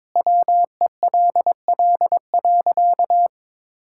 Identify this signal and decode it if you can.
Morse code, heard as a keyed tone, about 23 wpm, 700 Hz. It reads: WELL.
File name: practice / 22wpm / Easy Sentences with Hints - 22wpm